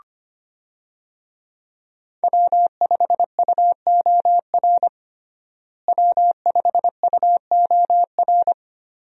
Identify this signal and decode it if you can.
Morse code, heard as a keyed tone, about 25 wpm, 700 Hz. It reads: W5UOR W5UOR
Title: USA Call Signs - 5 Characters - Repeated 3x - 25wpm